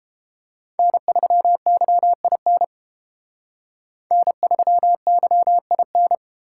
Morse code, heard as a keyed tone, about 28 wpm, 700 Hz. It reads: N3YIN N3YIN